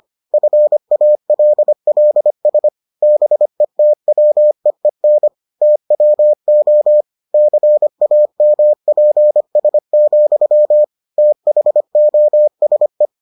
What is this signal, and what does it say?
Morse code, heard as a keyed tone, about 25 wpm, 600 Hz. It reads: FALLS BETWEEN TWO CAMPS, THOSE